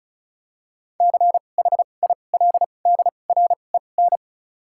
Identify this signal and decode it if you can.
Morse code, heard as a keyed tone, about 36 wpm, spaced as for 25 wpm, 700 Hz. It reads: CHILDREN
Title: 2x Character Spacing - Top 500 Words - 35wpm